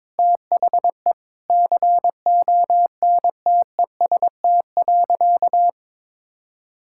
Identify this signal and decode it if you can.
Morse code, heard as a keyed tone, about 22 wpm, 700 Hz. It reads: THE CONTEST.
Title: Sentences from Top 1600 Words 22wpm